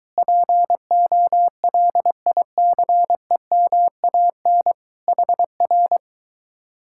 Morse code, heard as a keyed tone, about 22 wpm, 700 Hz. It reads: POLICEMAN HR